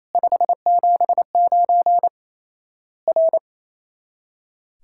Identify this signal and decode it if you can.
Morse code, heard as a keyed tone, about 28 wpm, 700 Hz. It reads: 579 R
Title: Worked All Europe DX Contest - QTC Exchange - 28wpm